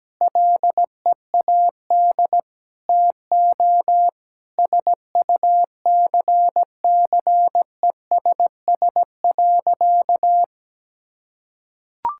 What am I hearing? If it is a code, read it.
Morse code, heard as a keyed tone, about 17 wpm, 700 Hz. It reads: LEAD TO SUCCESS.
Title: Sentences from Top 1000 Words - Review Part 6 - 17wpm